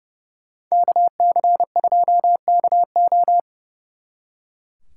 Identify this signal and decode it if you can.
Morse code, heard as a keyed tone, about 38 wpm, 700 Hz. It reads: KC2KO